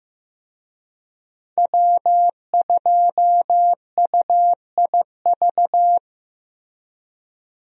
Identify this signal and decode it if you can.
Morse code, heard as a keyed tone, about 15 wpm, 700 Hz. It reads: W2UIV
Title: USA Call Signs - 5 Characters - Repeated 3x - 15wpm